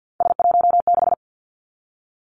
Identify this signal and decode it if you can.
Morse code, heard as a keyed tone, about 50 wpm, 700 Hz. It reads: S16
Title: Three Challenging Letter-Number Combinations 50wpm